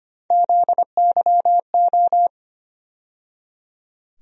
Morse code, heard as a keyed tone, about 25 wpm, 700 Hz. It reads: ZYO
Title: USA Call Sign Postfixes - 3 Letters 25wpm